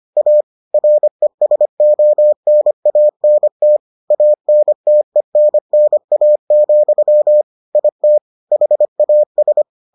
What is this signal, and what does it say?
Morse code, heard as a keyed tone, about 25 wpm, 600 Hz. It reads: A RESONANT ANTENNA, IT HAS